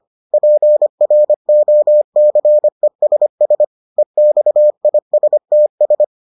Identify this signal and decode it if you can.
Morse code, heard as a keyed tone, about 25 wpm, 600 Hz. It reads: PROCESS EXISTS